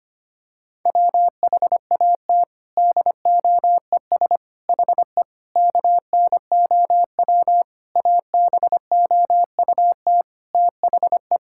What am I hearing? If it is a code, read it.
Morse code, heard as a keyed tone, about 25 wpm, 700 Hz. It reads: WHAT DOES HE KNOW ABOUT THE
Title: Sentences from Top 1600 Words 25wpm